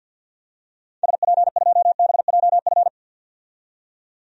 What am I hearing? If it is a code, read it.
Morse code, heard as a keyed tone, about 49 wpm, 700 Hz. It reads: SP2BJF
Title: DXCC-100 Call Signs - 6 Characters - 50wpm